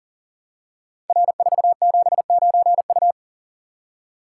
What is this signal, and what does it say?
Morse code, heard as a keyed tone, about 42 wpm, 700 Hz. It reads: R479U
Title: Code Groups - 5 Letter:Numbers - 40wpm